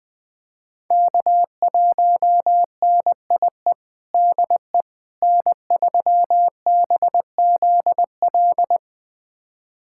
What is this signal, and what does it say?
Morse code, heard as a keyed tone, about 20 wpm, 700 Hz. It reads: K1NIE DE N3BZL
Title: USA Callsign DE Callsign - 5 Characters - 20wpm